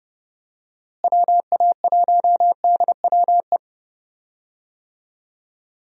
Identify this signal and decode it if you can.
Morse code, heard as a keyed tone, about 30 wpm, 700 Hz. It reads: WA1DWE
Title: USA Call Signs - 6 Characters - Repeated 3x - 30wpm